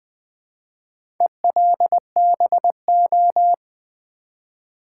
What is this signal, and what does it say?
Morse code, heard as a keyed tone, about 20 wpm, 700 Hz. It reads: ELBO
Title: Build A Word - 1 Letter At A Time - 5-Letter Words - 20wpm